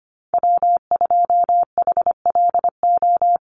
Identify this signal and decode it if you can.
Morse code, heard as a keyed tone, about 25 wpm, 700 Hz. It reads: W2HLO
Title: USA Call Signs 25wpm